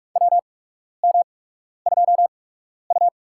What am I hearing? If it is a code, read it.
Morse code, heard as a keyed tone, about 44 wpm, spaced as for 12 wpm, 700 Hz. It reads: WM2U
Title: USA Call Signs - 4 Characters - 8x Character Spacing - 45wpm